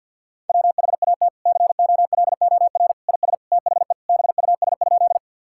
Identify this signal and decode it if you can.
Morse code, heard as a keyed tone, about 49 wpm, 700 Hz. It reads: WHAT COLOR IS THE BUS?
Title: Sentences from Top 1000 Words - Review Part 6 - 50wpm